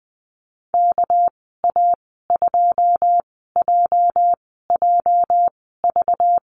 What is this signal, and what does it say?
Morse code, heard as a keyed tone, about 20 wpm, spaced as for 14 wpm, 700 Hz. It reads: KA2JJV
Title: 207 Lesson 33g - Letter j - Call Signs 20wpm